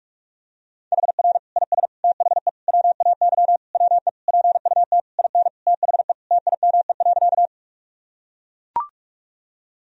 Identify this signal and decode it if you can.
Morse code, heard as a keyed tone, about 41 wpm, 700 Hz. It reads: HR IS THE WAY WE PUT IN THE TIME.
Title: Sentences from Top 200 Words and Abbreviations 45wpm